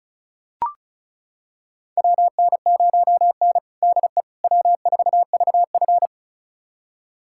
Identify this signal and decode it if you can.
Morse code, heard as a keyed tone, about 36 wpm, 700 Hz. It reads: WN0N DE W4VF
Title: USA Callsign DE Callsign - 4 Characters - 35wpm